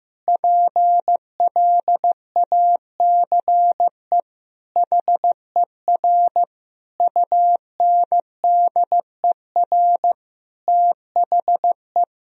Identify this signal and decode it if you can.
Morse code, heard as a keyed tone, about 15 wpm, 700 Hz. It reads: PLACE HER UNDER THE
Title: Sentences from Top 1000 Words - Review Part 1 - 15wpm